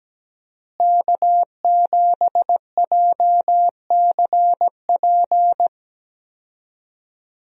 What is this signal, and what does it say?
Morse code, heard as a keyed tone, about 17 wpm, 700 Hz. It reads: K7JCP
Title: USA Call Signs - 5 Characters - Repeated 3x - 17wpm